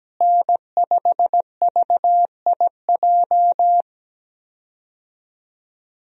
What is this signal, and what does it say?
Morse code, heard as a keyed tone, about 17 wpm, 700 Hz. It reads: N5VIJ
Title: USA Call Signs - 5 Characters - Repeated 2x - 17wpm